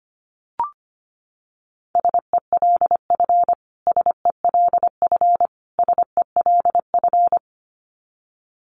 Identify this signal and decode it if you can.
Morse code, heard as a keyed tone, about 25 wpm, 700 Hz. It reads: SELF SELF SELF
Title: Top 500 Words Repeated 25wpm